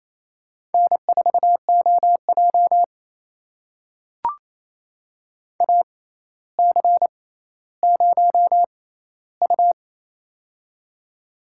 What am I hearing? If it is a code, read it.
Morse code, heard as a keyed tone, about 28 wpm, spaced as for 14 wpm, 700 Hz. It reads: N4OJ AC0U
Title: USA Call Signs - 4 Characters - 6x Character Spacing - 28wpm